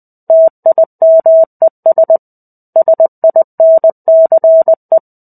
Morse code, heard as a keyed tone, about 20 wpm, 650 Hz. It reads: TIMES SINCE
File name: 05-Chapter-05 20 WPM